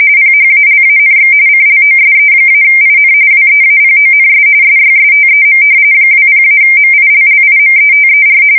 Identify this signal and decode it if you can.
RTTY, about 45.45 baud. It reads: PHALANX CLOSE-IN WEAPONS SYSTEM 20-MILLIMETER
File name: fsk170 / battleshipnj